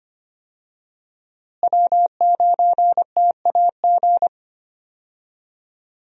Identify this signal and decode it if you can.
Morse code, heard as a keyed tone, about 25 wpm, 700 Hz. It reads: W9TAG
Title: USA Call Signs - 5 Characters - Repeated 3x - 25wpm